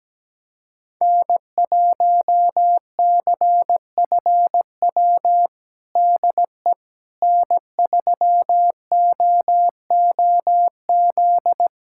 Morse code, heard as a keyed tone, about 17 wpm, 700 Hz. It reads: N1CFW DE N3OOZ
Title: USA Callsign DE Callsign 17wpm